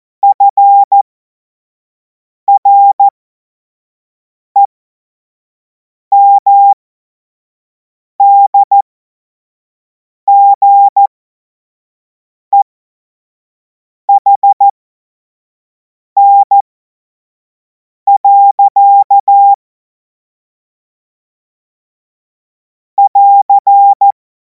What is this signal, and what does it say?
Morse code, heard as a keyed tone, about 14 wpm, spaced as for 5 wpm, 800 Hz. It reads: FREMDGEHN. <AR>
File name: Morsekurs / Witze / Witz 24. Februar 2026 25 F70